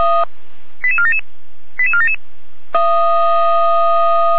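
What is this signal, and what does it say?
Five-tone selective call (ZVEI), heard as quick runs of five tones, tones of about 80 ms. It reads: 88477 88477
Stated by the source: ZVEI five-tone call received by SDR, from feuerwehr-orlamuende.org (fire brigade Orlamünde).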